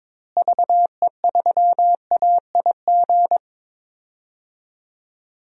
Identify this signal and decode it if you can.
Morse code, heard as a keyed tone, about 22 wpm, 700 Hz. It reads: VE3AIG
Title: DXCC-100 Call Signs - 6 Characters - Repeated 3x - 22wpm